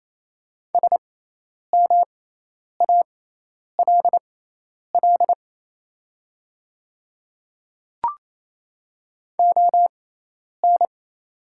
Morse code, heard as a keyed tone, about 27 wpm, spaced as for 10 wpm, 700 Hz. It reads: SMALL ON
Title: 6x Character Spacing - Top 200 Words - 28wpm